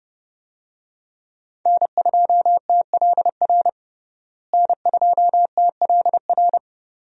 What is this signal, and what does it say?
Morse code, heard as a keyed tone, about 30 wpm, 700 Hz. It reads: N2TLR N2TLR
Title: USA Call Signs - 5 Characters - Repeated 2x - 30wpm